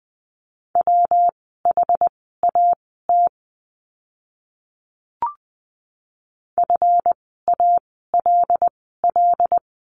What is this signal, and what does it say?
Morse code, heard as a keyed tone, about 20 wpm, spaced as for 14 wpm, 700 Hz. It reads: WHAT FALL
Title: 139 Lesson 23f - Letter f - Words 20wpm